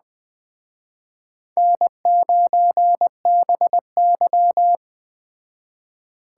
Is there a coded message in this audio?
Yes: Morse code from 1.6 to 4.8 s, heard as a keyed tone, about 20 wpm, 700 Hz. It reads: N9BY